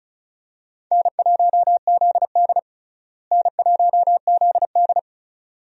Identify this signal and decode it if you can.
Morse code, heard as a keyed tone, about 34 wpm, 700 Hz. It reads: N1ZD N1ZD
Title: USA Call Signs - 4 Characters - Repeated 2x - 35wpm